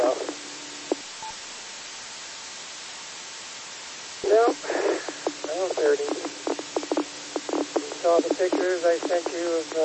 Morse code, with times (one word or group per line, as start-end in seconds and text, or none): none